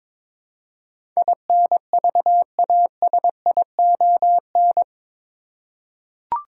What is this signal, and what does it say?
Morse code, heard as a keyed tone, about 21 wpm, 700 Hz. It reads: INVASION
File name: practice / 22wpm / Eight Letter Words - 22wpm